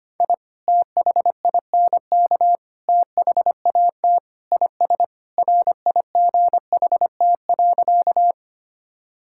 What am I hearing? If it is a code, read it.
Morse code, heard as a keyed tone, about 25 wpm, 700 Hz. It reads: I THINK THAT IS RIGHT.
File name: practice / 25wpm / Sentences with 2-Word Phrases - 25wpm